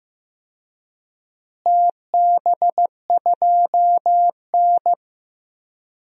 Morse code, heard as a keyed tone, about 15 wpm, 700 Hz. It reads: TB2N
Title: DXCC-100 Call Signs - 4 Characters - Repeated 3x - 15wpm